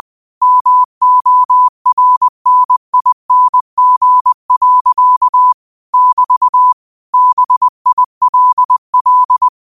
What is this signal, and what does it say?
Morse code, heard as a keyed tone, about 20 wpm, 1000 Hz. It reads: MORNING. <BT> BILL